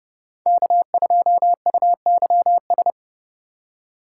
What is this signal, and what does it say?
Morse code, heard as a keyed tone, about 30 wpm, 700 Hz. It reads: K2UYS